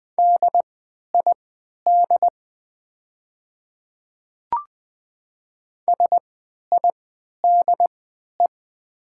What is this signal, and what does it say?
Morse code, heard as a keyed tone, about 21 wpm, spaced as for 12 wpm, 700 Hz. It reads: DID SIDE
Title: 3x Character Spacing - Top 100 Words - 20wpm